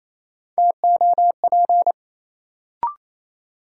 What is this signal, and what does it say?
Morse code, heard as a keyed tone, about 28 wpm, 700 Hz. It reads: TOP
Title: Sentences from Top 700 Words - New Word Drill - 28wpm